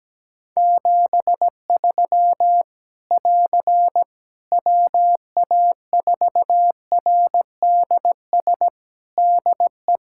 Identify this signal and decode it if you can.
Morse code, heard as a keyed tone, about 17 wpm, 700 Hz. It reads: 73 <AR> WA4RDS DE
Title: K1USN SST QSO Practice 17wpm